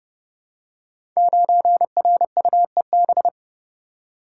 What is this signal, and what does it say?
Morse code, heard as a keyed tone, about 30 wpm, 700 Hz. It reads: 9RUEB
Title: Code Groups - 5 Letter:Numbers - 30wpm